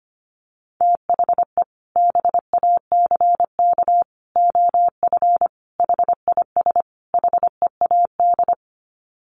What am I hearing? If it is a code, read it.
Morse code, heard as a keyed tone, about 25 wpm, 700 Hz. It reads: THE BACK OF HIS HEAD
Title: Sets of 5 Words Encore 25wpm